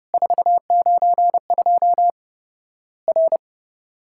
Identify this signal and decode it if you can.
Morse code, heard as a keyed tone, about 32 wpm, 700 Hz. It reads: 492 R
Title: Worked All Europe DX Contest - QTC Exchange - 30wpm